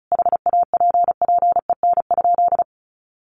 Morse code, heard as a keyed tone, about 34 wpm, 700 Hz. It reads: HAPPEN?
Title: Sentences from Top 400 Words 35wpm